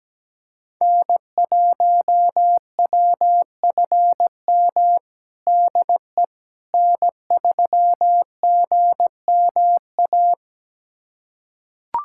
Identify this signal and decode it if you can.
Morse code, heard as a keyed tone, about 17 wpm, 700 Hz. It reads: N1WFM DE N3GMA
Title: USA Callsign DE Callsign - 5 Characters - 17wpm